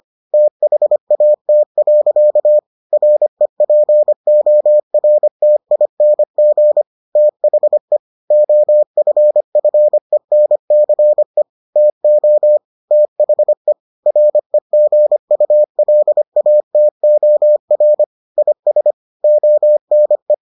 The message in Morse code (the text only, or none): THAT. REPORTING THE OFFENCE TO THE REGULATOR IS ONE